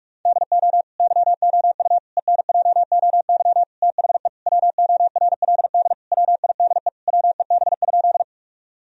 Morse code, heard as a keyed tone, about 47 wpm, 700 Hz. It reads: DO YOU ENJOY THE WORLD WIDE WEB?
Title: Sentences from Top 1000 Words - Review Part 4 - 45wpm